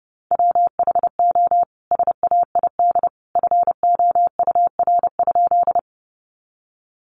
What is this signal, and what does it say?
Morse code, heard as a keyed tone, about 28 wpm, 700 Hz. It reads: WHO SAID FOUR?
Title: Sentences from Top 300 Words 30wpm